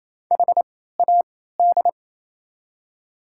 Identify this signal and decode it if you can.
Morse code, heard as a keyed tone, about 27 wpm, spaced as for 16 wpm, 700 Hz. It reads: HAD
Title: 3x Character Spacing - Top 300 Words - 28wpm